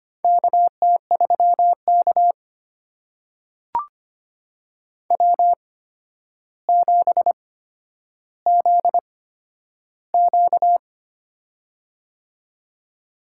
Morse code, heard as a keyed tone, about 25 wpm, spaced as for 11 wpm, 700 Hz. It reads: KT3K W7ZQ
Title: USA Call Signs - 4 Characters - 8x Character Spacing - 25wpm